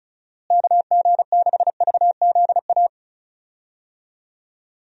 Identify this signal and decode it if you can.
Morse code, heard as a keyed tone, about 36 wpm, 700 Hz. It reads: KG6VZA
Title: USA Call Signs - 6 Characters - Repeated 3x - 35wpm